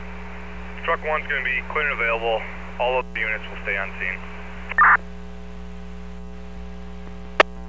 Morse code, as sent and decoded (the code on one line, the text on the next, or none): none